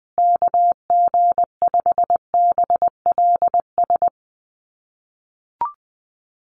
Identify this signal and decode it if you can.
Morse code, heard as a keyed tone, about 20 wpm, 700 Hz. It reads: KG5BLS
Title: USA Call Signs Repeated 2x 20wpm